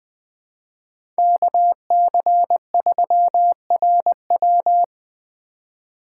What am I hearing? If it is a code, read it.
Morse code, heard as a keyed tone, about 20 wpm, 700 Hz. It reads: KC3RW
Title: 2023 POTA Activator Call Signs - 20wpm